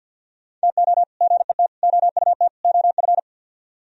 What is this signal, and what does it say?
Morse code, heard as a keyed tone, about 64 wpm, 700 Hz. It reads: TO GET OUT OF